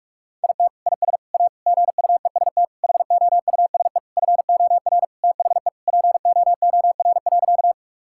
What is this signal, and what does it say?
Morse code, heard as a keyed tone, about 41 wpm, 700 Hz. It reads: IT IS A GUEST HOUSE FOR THE POOR.
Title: Sentences from Top 1600 Words 45wpm